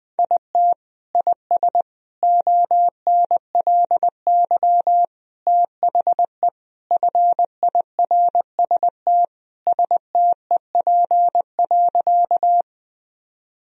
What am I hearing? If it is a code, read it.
Morse code, heard as a keyed tone, about 20 wpm, 700 Hz. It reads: IT IS ONLY THE FIRST STEP.
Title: Sentences from Top 1000 Words - Review Part 2 - 20wpm